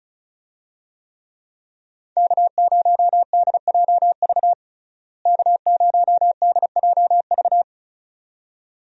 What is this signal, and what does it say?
Morse code, heard as a keyed tone, about 36 wpm, 700 Hz. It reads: K0DJV K0DJV
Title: USA Call Signs - 5 Characters - Repeated 2x - 35wpm